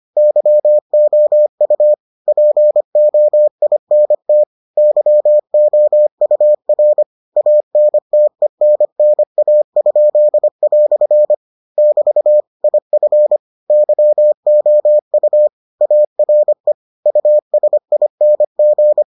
Morse code, heard as a keyed tone, about 25 wpm, 600 Hz. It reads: YOU POINT YOUR ANTENNA?" <BT> IF YOU ARE USING